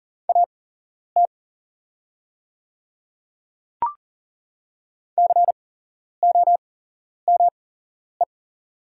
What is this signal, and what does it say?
Morse code, heard as a keyed tone, about 40 wpm, spaced as for 11 wpm, 700 Hz. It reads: AT COME
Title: 8x Character Spacing - Top 500 Words - 40wpm